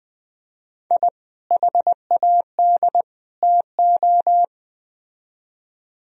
Morse code, heard as a keyed tone, about 20 wpm, 700 Hz. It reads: I HAD TO
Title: Speed-Racing - Sets of 3 Words 17wpm